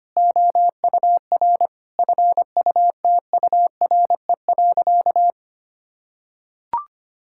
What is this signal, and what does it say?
Morse code, heard as a keyed tone, about 25 wpm, 700 Hz. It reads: OUR FUTURE.
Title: Sentences from Top 1600 Words 25wpm